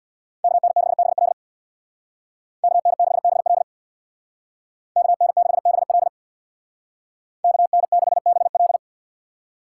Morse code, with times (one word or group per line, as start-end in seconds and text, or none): 0.4-1.3 KN6BL
2.6-3.6 KN6BL
5.0-6.1 KN6BL
7.4-8.8 KN6BL